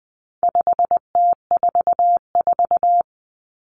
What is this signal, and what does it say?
Morse code, heard as a keyed tone, about 20 wpm, 700 Hz. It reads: 5T44
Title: Common Cut Numbers - 4-Digit 20wpm